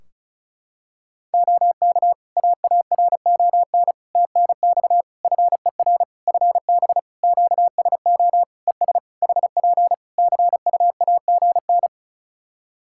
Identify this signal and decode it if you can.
Morse code, heard as a keyed tone, about 36 wpm, 700 Hz. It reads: OK AARON TNX FER FB QSO ES HP CUAGN
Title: K1USN SST QSO Practice 35wpm